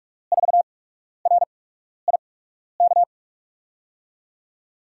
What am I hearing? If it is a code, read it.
Morse code, heard as a keyed tone, about 44 wpm, spaced as for 12 wpm, 700 Hz. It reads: 4RIK
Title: USA Call Signs - 5 Characters - 8x Character Spacing - 45wpm